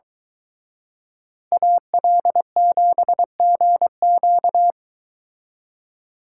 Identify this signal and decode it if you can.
Morse code, heard as a keyed tone, about 23 wpm, 700 Hz. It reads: AL7GQ